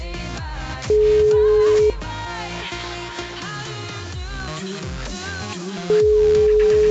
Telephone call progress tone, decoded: ringback tone

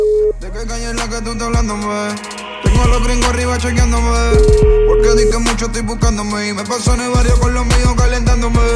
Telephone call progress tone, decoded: ringback tone

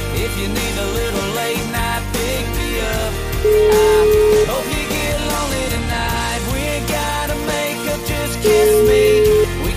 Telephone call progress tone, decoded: ringback tone

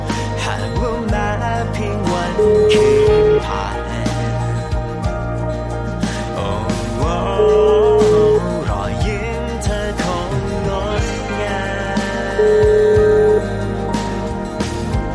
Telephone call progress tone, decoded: ringback tone

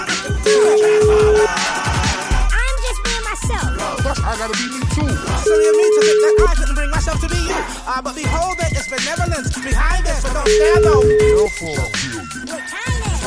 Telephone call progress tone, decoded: ringback tone

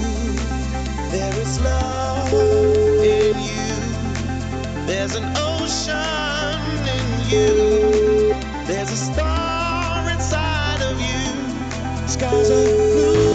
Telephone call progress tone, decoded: ringback tone

